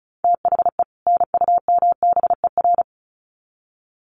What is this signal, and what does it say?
Morse code, heard as a keyed tone, about 34 wpm, 700 Hz. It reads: THE NUMBER
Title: Sets of 2 Words 35wpm